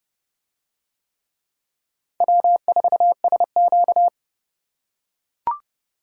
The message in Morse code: W4SQ

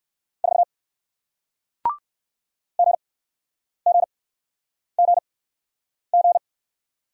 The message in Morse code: 4 GGGG

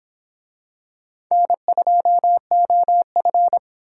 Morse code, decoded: N2OF

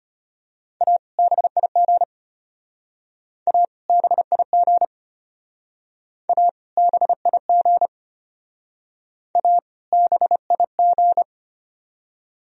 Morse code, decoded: A BIG A BIG A BIG A BIG

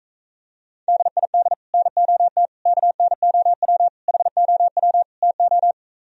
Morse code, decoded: DID NOT KNOW HOW TO

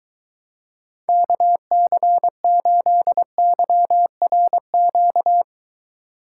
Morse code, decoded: KC8YRQ

